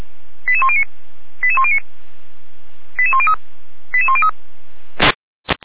Five-tone selective call (ZVEI), heard as quick runs of five tones, tones of about 70 ms. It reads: 88109 88109 88103 88103